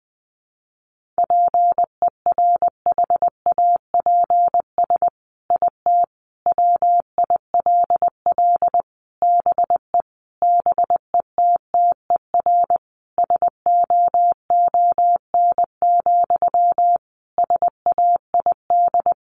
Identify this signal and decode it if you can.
Morse code, heard as a keyed tone, about 20 wpm, 700 Hz. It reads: PERHAPS IT WILL BE BETTER SOON, SAID
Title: Sentences from Top 1500 Words 20wpm